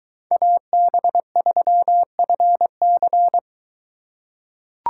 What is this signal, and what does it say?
Morse code, heard as a keyed tone, about 22 wpm, 700 Hz. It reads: AB3FC